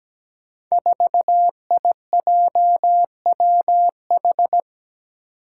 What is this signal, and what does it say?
Morse code, heard as a keyed tone, about 17 wpm, 700 Hz. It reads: 4IJWH
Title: Five Challenging Letter-Number Combinations 17wpm